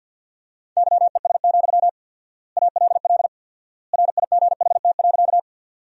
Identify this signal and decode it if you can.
Morse code, heard as a keyed tone, about 50 wpm, 700 Hz. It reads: YES, ALL RIGHT.
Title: 4x Word-Spacing - Sentences from Top 500 Words - 50wpm